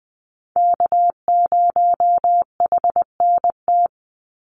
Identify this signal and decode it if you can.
Morse code, heard as a keyed tone, about 19 wpm, 700 Hz. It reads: K0HNT
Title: USA Call Signs 20wpm